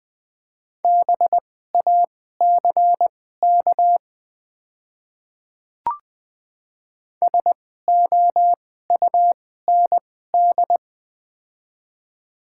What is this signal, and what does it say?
Morse code, heard as a keyed tone, about 20 wpm, spaced as for 14 wpm, 700 Hz. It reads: BACK SOUND
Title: 2x Character Spacing - Top 200 Words - 20wpm